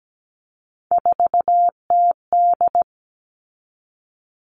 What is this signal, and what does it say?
Morse code, heard as a keyed tone, about 17 wpm, 700 Hz. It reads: 4TD
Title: All Cut Numbers - 3-Digit 17wpm